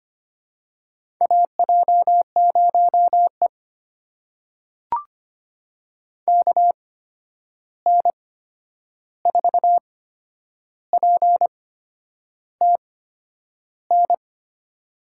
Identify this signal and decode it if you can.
Morse code, heard as a keyed tone, about 25 wpm, spaced as for 9 wpm, 700 Hz. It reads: AJ0E KN4PTN
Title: USA Call Signs - 8x Character Spacing - 25wpm